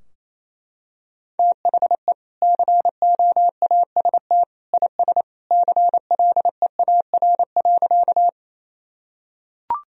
Morse code, heard as a keyed tone, about 28 wpm, 700 Hz. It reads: THE COAST IS CLEAR.
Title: Sentences from Top 1000 Words - Review Part 4 - 28wpm